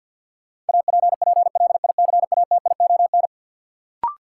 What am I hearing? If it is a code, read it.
Morse code, heard as a keyed tone, about 48 wpm, 700 Hz. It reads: APPLICATION
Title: Sentences from Top 1200 Words - New Word Drill - 50wpm